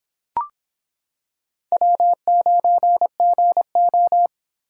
Morse code, heard as a keyed tone, about 26 wpm, 700 Hz. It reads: W9GO